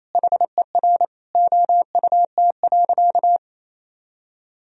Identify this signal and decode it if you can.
Morse code, heard as a keyed tone, about 28 wpm, 700 Hz. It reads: HER OUT.
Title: Sentences from Top 1000 Words - Review Part 2 - 28wpm